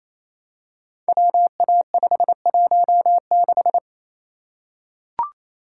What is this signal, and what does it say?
Morse code, heard as a keyed tone, about 27 wpm, 700 Hz. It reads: WA516
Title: Code Groups - 5 Letter:Numbers - 28wpm